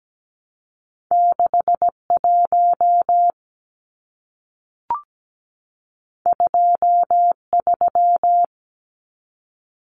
Morse code, heard as a keyed tone, about 17 wpm, 700 Hz. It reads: 61 23
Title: Numbers - 2-Digit 17wpm